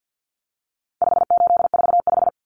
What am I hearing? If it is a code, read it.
Morse code, heard as a keyed tone, about 48 wpm, 700 Hz. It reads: 5845
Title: Common Cut Numbers - 4-Digit 50wpm